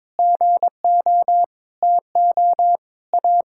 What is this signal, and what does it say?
Morse code, heard as a keyed tone, about 22 wpm, 700 Hz. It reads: GO TO A
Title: Sentences from Top 1000 Words - Review Part 5 - 22wpm